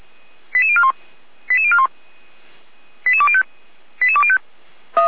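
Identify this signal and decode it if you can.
Five-tone selective call (ZVEI), heard as quick runs of five tones, tones of about 70 ms. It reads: 88041 88041 88205 88205
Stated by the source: ZVEI five-tone call 89502 received with an SDR, archived on the Feuerwehr Orlamünde audio server.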